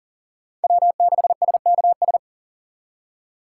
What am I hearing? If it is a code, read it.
Morse code, heard as a keyed tone, about 52 wpm, 700 Hz. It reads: W6SKS